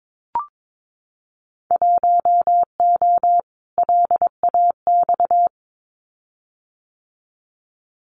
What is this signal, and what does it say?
Morse code, heard as a keyed tone, about 22 wpm, 700 Hz. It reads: 1O LAX
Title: Winter Field Day Exchanges 22wpm